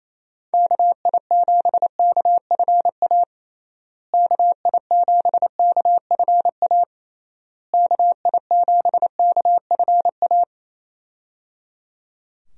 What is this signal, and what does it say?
Morse code, heard as a keyed tone, about 28 wpm, 700 Hz. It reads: KI7KFA KI7KFA KI7KFA
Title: USA Call Signs - 6 Characters - Repeated 3x - 28wpm